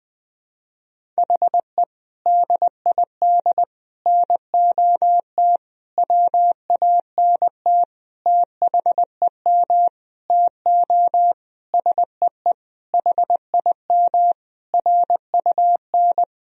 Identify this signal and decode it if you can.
Morse code, heard as a keyed tone, about 20 wpm, 700 Hz. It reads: HE DID NOT WANT THEM TO SEE HIM RUN
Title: Sentences from Top 1000 Words - Review Part 1 - 20wpm